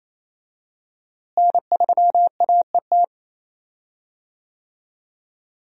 Morse code, heard as a keyed tone, about 27 wpm, 700 Hz. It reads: N3AET
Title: USA Call Signs - 5 Characters - Repeated 2x - 28wpm